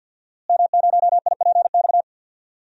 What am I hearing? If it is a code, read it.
Morse code, heard as a keyed tone, about 53 wpm, 700 Hz. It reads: M0IPX
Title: DXCC-100 Call Signs - 5 Characters - Repeated 2x - 50wpm